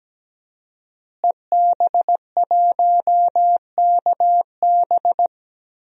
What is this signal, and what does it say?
Morse code, heard as a keyed tone, about 17 wpm, 700 Hz. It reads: EB1KB
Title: DXCC-100 Call Signs - 5 Characters - Repeated 3x - 17wpm